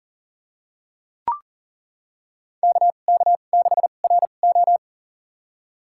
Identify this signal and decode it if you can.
Morse code, heard as a keyed tone, about 40 wpm, spaced as for 29 wpm, 700 Hz. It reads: KK6RO
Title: USA Call Signs - 5 Characters - 2x Character Spacing - 40wpm